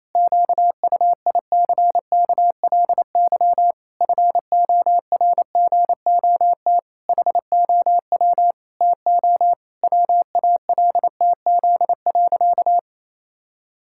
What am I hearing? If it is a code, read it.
Morse code, heard as a keyed tone, about 28 wpm, 700 Hz. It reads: QUICKLY FORGOT HOW TO WALTZ.
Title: Pangram Challenge v2 - 28wpm